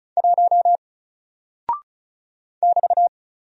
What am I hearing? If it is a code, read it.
Morse code, heard as a keyed tone, about 36 wpm, 700 Hz. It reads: 1 <BT>
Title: Single Character (Letter:Number:Punctuation:Prosign) Alt 35wpm